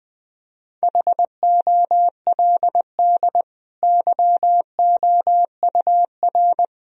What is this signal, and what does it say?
Morse code, heard as a keyed tone, about 21 wpm, 700 Hz. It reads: HOLD YOUR
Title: Sentences from Top 1000 Words - Review Part 2 - 20wpm